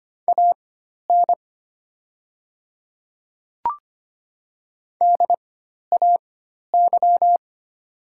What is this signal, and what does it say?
Morse code, heard as a keyed tone, about 25 wpm, spaced as for 12 wpm, 700 Hz. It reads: AN DAY
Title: 4x Character Spacing - Top 100 Words - 25wpm